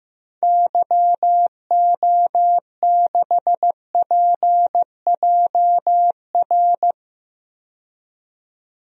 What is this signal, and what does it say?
Morse code, heard as a keyed tone, about 15 wpm, 700 Hz. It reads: YO6PJR
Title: DXCC-100 Call Signs - 6 Characters - 15wpm